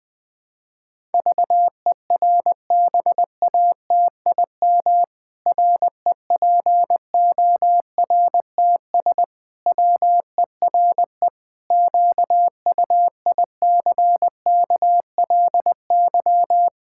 Morse code, heard as a keyed tone, about 20 wpm, 700 Hz. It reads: VERBATIM REPORTS WERE QUICKLY
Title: Pangram Challenge v2 - 20wpm